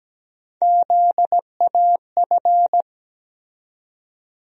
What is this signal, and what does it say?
Morse code, heard as a keyed tone, about 17 wpm, 700 Hz. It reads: ZAF
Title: USA Call Sign Postfixes - 3 Letters 17wpm